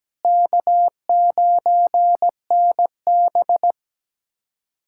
Morse code, heard as a keyed tone, about 17 wpm, 700 Hz. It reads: K9NB
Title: USA Callsign DE Callsign - 4 Characters - 17wpm